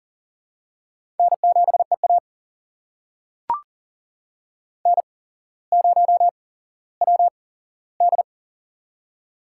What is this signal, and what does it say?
Morse code, heard as a keyed tone, about 40 wpm, spaced as for 17 wpm, 700 Hz. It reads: N7EA N0WD
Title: USA Call Signs - 4 Characters - 8x Character Spacing - 40wpm